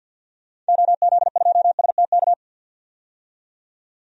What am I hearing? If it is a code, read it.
Morse code, heard as a keyed tone, about 50 wpm, 700 Hz. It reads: OZ2STK
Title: DXCC-100 Call Signs - 6 Characters - 50wpm